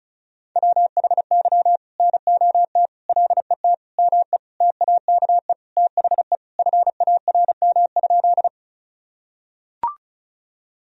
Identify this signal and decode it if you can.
Morse code, heard as a keyed tone, about 34 wpm, 700 Hz. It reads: WHY NOT LET ME TAKE THE FARM?
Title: Sentences from Top 1000 Words - Review Part 2 - 35wpm